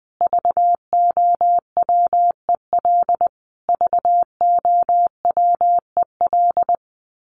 Morse code, heard as a keyed tone, about 20 wpm, 700 Hz. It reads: VOWEL VOWEL
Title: Top 500 Words Repeated 20wpm